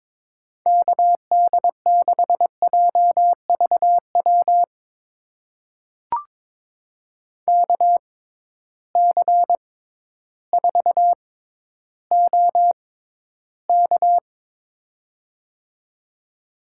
Morse code, heard as a keyed tone, about 22 wpm, spaced as for 12 wpm, 700 Hz. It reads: KD6JVW KC4OK
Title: USA Call Signs - 6x Character Spacing - 22wpm